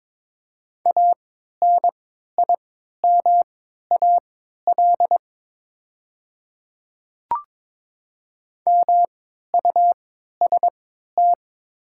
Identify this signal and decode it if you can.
Morse code, heard as a keyed tone, about 22 wpm, spaced as for 13 wpm, 690 Hz. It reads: ANIMAL MUST
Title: 3x Character Spacing - Top 200 Words - 22wpm